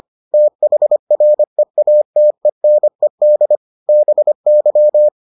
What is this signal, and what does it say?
Morse code, heard as a keyed tone, about 25 wpm, 600 Hz. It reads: THREATENED BY